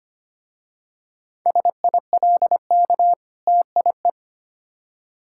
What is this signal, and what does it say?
Morse code, heard as a keyed tone, about 25 wpm, 700 Hz. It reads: SILK TIE
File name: practice / 25wpm / Difficult Words with Hints - 25wpm